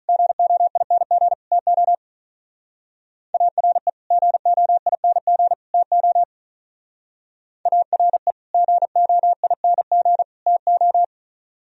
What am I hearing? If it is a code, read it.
Morse code, heard as a keyed tone, about 42 wpm, 700 Hz. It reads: GOING TO ARE GOING TO ARE GOING TO
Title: Speed-Racing - Sets of 3 Words 35wpm